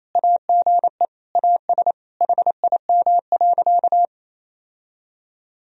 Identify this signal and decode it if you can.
Morse code, heard as a keyed tone, about 28 wpm, 700 Hz. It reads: AGE AS HIM.
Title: Sentences from Top 1000 Words - Review Part 3 - 28wpm